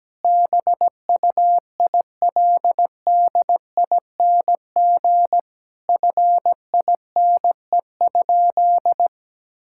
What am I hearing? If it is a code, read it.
Morse code, heard as a keyed tone, about 17 wpm, 700 Hz. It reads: BUILDING FINE?
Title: Sentences from Top 1000 Words - Review Part 5 - 17wpm